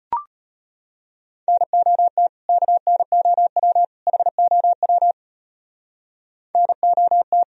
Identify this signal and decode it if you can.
Morse code, heard as a keyed tone, about 31 wpm, 700 Hz. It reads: NOT KNOW HOW NOT
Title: Speed-Racing - Sets of 3 Words 25wpm